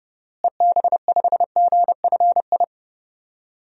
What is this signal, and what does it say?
Morse code, heard as a keyed tone, about 30 wpm, 700 Hz. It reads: EB5GFI
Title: DXCC-100 Call Signs - 6 Characters - Repeated 2x - 30wpm